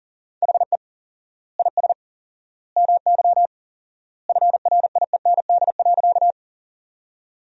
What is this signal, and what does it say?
Morse code, heard as a keyed tone, about 40 wpm, 700 Hz. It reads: HE IS MY FRIEND.
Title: 4x Word-Spacing - Sentences from Top 300 Words - 40wpm